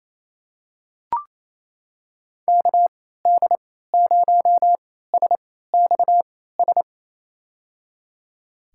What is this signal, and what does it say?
Morse code, heard as a keyed tone, about 27 wpm, spaced as for 16 wpm, 700 Hz. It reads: KD0SXS
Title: USA Call Signs - 6 Characters - 3x Character Spacing - 28wpm